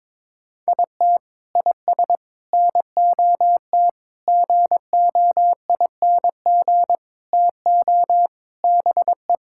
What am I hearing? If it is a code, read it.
Morse code, heard as a keyed tone, about 22 wpm, 700 Hz. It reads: IT IS NOT GOING TO BE
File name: practice / 22wpm / Sentences with 4-Word Phrases - 22wpm